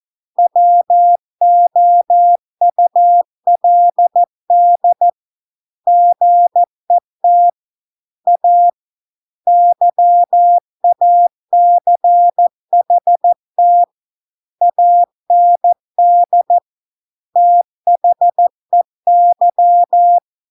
Morse code, heard as a keyed tone, about 14 wpm, 700 Hz. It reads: WOULD GET A YACHT AND THEY